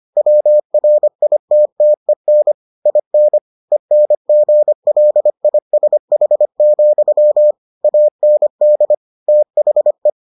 Morse code, heard as a keyed tone, about 25 wpm, 600 Hz. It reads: WRITTEN IN ENGLISH, AND THE